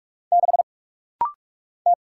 Morse code, heard as a keyed tone, about 46 wpm, 700 Hz. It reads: 6 T
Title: Single Letter-Number Alt - Rapid-Fire 45wpm